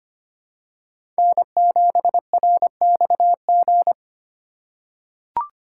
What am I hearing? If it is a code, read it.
Morse code, heard as a keyed tone, about 25 wpm, 700 Hz. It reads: N7RXG
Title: USA Call Signs - 3x Character Spacing - 25wpm